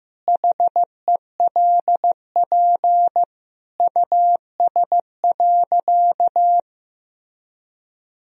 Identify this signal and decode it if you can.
Morse code, heard as a keyed tone, about 15 wpm, 700 Hz. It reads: HELP US.
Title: Sentences from Top 1000 Words - Review Part 3 - 15wpm